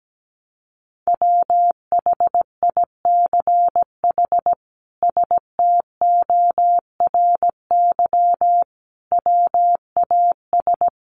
Morse code, heard as a keyed tone, about 17 wpm, 700 Hz. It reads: WHICH STORY WAS
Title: Sentences from Top 300 Words 17wpm